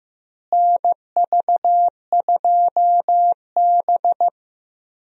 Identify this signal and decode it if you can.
Morse code, heard as a keyed tone, about 15 wpm, 700 Hz. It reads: NV2B
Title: USA Call Signs - 4 Characters - Repeated 2x - 15wpm